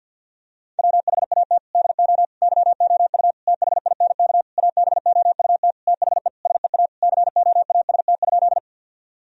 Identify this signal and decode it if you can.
Morse code, heard as a keyed tone, about 50 wpm, 700 Hz. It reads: WHAT DO YOU THINK ABOUT THE SEA COAST?
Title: Sentences from Top 1000 Words - Review Part 4 - 50wpm